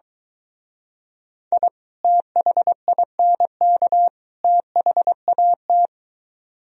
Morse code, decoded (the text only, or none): I THINK THAT